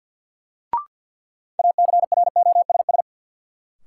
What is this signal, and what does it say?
Morse code, heard as a keyed tone, about 47 wpm, 700 Hz. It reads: ACROSS
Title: Sets of 2 Words - New Word Drill - 50wpm